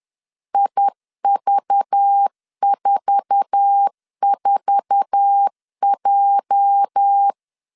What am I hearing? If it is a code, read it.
Morse code, heard as a keyed tone, about 10 wpm, 800 Hz. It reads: IV44J
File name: LEZIONI / 8 YJ45 / 8 YJ4 10wpm 2633